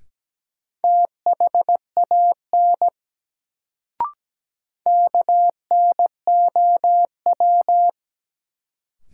Morse code, heard as a keyed tone, about 17 wpm, 700 Hz. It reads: THAN KNOW
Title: Sentences from Top 100 Words - New Word Drill - 17wpm